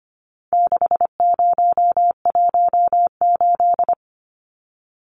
Morse code, heard as a keyed tone, about 25 wpm, 700 Hz. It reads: 6018
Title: Numbers - 4-Digit 25wpm